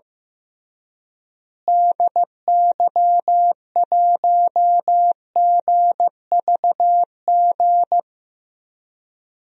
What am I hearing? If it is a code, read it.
Morse code, heard as a keyed tone, about 15 wpm, 700 Hz. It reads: DY1GVG
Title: DXCC-100 Call Signs - 6 Characters - Repeated 2x - 15wpm